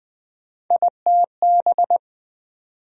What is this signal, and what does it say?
Morse code, heard as a keyed tone, about 20 wpm, 700 Hz. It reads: ITB
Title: USA Call Sign Postfixes - 3 Letters 20wpm